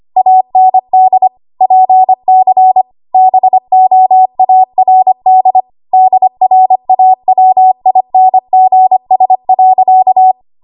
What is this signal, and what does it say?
Morse code, heard as a keyed tone, about 25 wpm, 750 Hz. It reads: AND PC BOARD DRAWINGS.
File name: Archive / 25 WPM / 25WPM